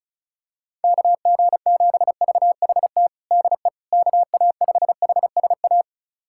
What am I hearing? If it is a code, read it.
Morse code, heard as a keyed tone, about 35 wpm, 700 Hz. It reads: KG7VHT DE KA5HSA
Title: USA Callsign DE Callsign - 6 Characters - 35wpm